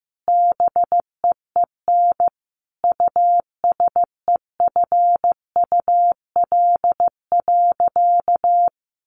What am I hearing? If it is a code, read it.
Morse code, heard as a keyed tone, about 15 wpm, 700 Hz. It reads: BEEN USEFUL.
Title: Sentences from Top 700 Words 15wpm